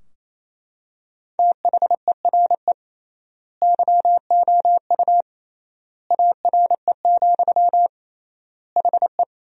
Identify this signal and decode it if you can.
Morse code, heard as a keyed tone, about 28 wpm, 700 Hz. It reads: THERE YOU ARE, HE
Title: 3x Word-Spacing - Sentences from Top 100 Words - 28wpm